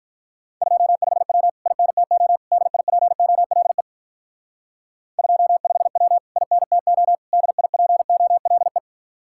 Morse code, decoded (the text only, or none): 25W INTO DIPOLE 25W INTO DIPOLE